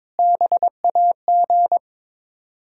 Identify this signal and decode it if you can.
Morse code, heard as a keyed tone, about 22 wpm, 700 Hz. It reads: BAG
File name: Difficult Words with Hints - 22wpm